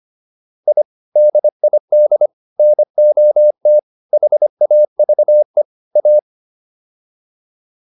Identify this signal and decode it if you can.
Morse code, heard as a keyed tone, about 25 wpm, 600 Hz. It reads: I DID NOT HAVE A